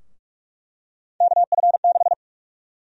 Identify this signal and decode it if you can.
Morse code, heard as a keyed tone, about 46 wpm, 700 Hz. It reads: KF6
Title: USA Call Sign Prefixes - 45wpm